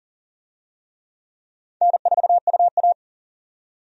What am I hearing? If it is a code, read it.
Morse code, heard as a keyed tone, about 40 wpm, 700 Hz. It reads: N4UA
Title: USA Call Signs - 4 Characters - 40wpm